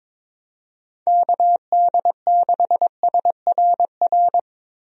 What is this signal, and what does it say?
Morse code, heard as a keyed tone, about 22 wpm, 700 Hz. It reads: KD6SRR
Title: USA Call Signs - 6 Characters - Repeated 3x - 22wpm